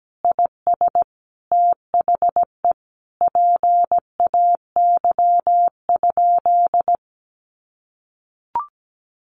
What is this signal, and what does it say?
Morse code, heard as a keyed tone, about 17 wpm, 700 Hz. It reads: IS THE PAY?
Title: Sentences from Top 700 Words 17wpm